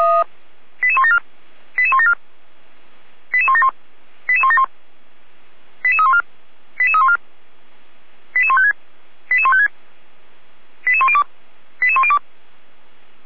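Five-tone selective call (ZVEI), heard as quick runs of five tones, tones of about 70 ms. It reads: 88174 88174 88171 88171 88314 88314 88156 88156 88102 88102
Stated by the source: ZVEI five-tone selective call received by SDR.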